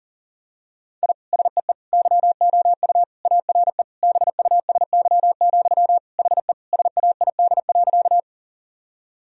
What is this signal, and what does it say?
Morse code, heard as a keyed tone, about 40 wpm, 700 Hz. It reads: I SEE YOU ARE BUSY, HE SAID.
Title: Sentences from Top 1000 Words - Review Part 3 - 40wpm